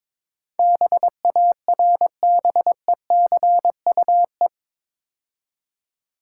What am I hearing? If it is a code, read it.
Morse code, heard as a keyed tone, about 22 wpm, 700 Hz. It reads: BARBECUE
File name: Easy Words - Speed-Racing with Hints - 22wpm